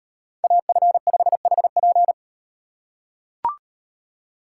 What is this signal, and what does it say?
Morse code, heard as a keyed tone, about 38 wpm, 700 Hz. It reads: AF5HP